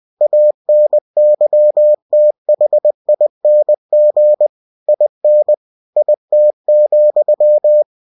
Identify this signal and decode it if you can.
Morse code, heard as a keyed tone, about 20 wpm, 600 Hz. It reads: ANYTHING IN IT,
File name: cw-books / winnie-the-pooh / winnie-20wpm-page-0051